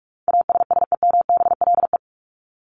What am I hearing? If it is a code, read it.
Morse code, heard as a keyed tone, about 44 wpm, 700 Hz. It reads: ASSEMBLE
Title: LY Suffix - Top 100 Words with Base Words 45wpm